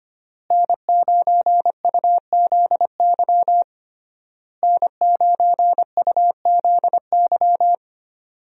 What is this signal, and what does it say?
Morse code, heard as a keyed tone, about 25 wpm, 700 Hz. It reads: N9UZY N9UZY
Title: USA Call Signs - 5 Characters - Repeated 3x - 25wpm